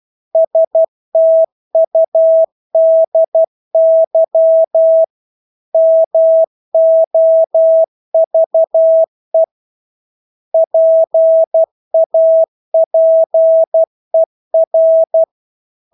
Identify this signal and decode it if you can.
Morse code, heard as a keyed tone, about 12 wpm, 650 Hz. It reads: STUDY MOVE PAPER
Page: Engelske ord 12 wpm